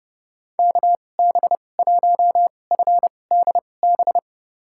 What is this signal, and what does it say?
Morse code, heard as a keyed tone, about 30 wpm, spaced as for 22 wpm, 700 Hz. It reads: KB1FDB
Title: USA Call Signs - 6 Characters - 2x Character Spacing - 30wpm